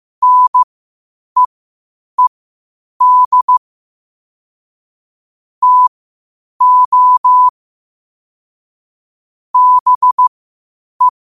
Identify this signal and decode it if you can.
Morse code, heard as a keyed tone, about 15 wpm, spaced as for 8 wpm, 1000 Hz. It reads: NEED TO BE
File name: qotdfiles / 8wpm